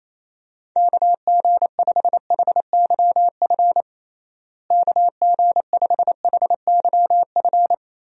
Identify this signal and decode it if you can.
Morse code, heard as a keyed tone, about 28 wpm, 700 Hz. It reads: KG5HYF KG5HYF
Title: USA Call Signs - 6 Characters - Repeated 3x - 28wpm